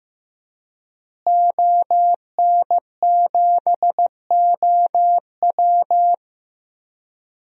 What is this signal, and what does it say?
Morse code, heard as a keyed tone, about 15 wpm, 700 Hz. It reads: ON7OW
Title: DXCC-100 Call Signs - 5 Characters - Repeated 3x - 15wpm